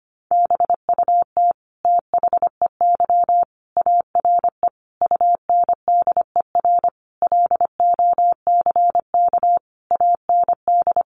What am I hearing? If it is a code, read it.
Morse code, heard as a keyed tone, about 25 wpm, 700 Hz. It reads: BUT THEY ARE UNDER LOCK AND
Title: Sentences from Top 1300 Words 25wpm